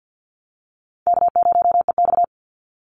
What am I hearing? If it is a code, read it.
Morse code, heard as a keyed tone, about 48 wpm, 700 Hz. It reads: K0EX
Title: Easy Call Signs Repeated 2x - 50wpm